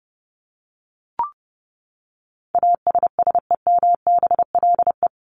Morse code, heard as a keyed tone, about 30 wpm, 700 Hz. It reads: ASSEMBLE
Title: LY Suffix - Top 100 Words with Base Words 30wpm